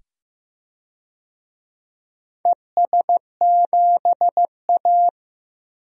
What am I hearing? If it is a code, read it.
Morse code, heard as a keyed tone, about 15 wpm, 700 Hz. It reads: ES7A
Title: DXCC-100 Call Signs - 4 Characters - 15wpm